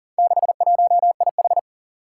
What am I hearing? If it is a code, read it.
Morse code, heard as a keyed tone, about 43 wpm, 700 Hz. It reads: 61IH
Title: Four Challenging Letter-Number Combinations 40wpm